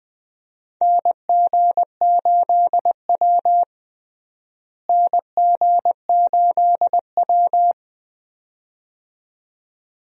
Morse code, heard as a keyed tone, about 20 wpm, 700 Hz. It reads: NG8W NG8W
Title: USA Call Signs - 4 Characters - Repeated 2x - 20wpm